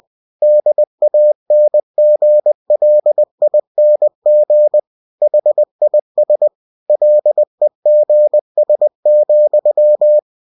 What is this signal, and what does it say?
Morse code, heard as a keyed tone, about 20 wpm, 600 Hz. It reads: DANGLING HIS LEGS,
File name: cw-books / winnie-the-pooh / winnie-20wpm-page-0094